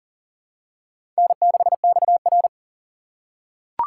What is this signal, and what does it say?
Morse code, heard as a keyed tone, about 40 wpm, 700 Hz. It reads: N6XR
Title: USA Call Signs - 2x Character Spacing - 40wpm